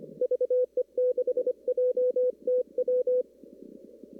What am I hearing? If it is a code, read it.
Morse code, heard as a keyed tone, about 25 wpm, 1500 Hz. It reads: VE6JTW
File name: CW / kt5j wpx cw-op-k5pi